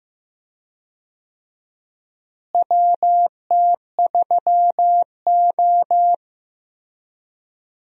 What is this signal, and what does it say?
Morse code, heard as a keyed tone, about 15 wpm, 700 Hz. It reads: WT3O
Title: USA Call Signs - 4 Characters - Repeated 2x - 15wpm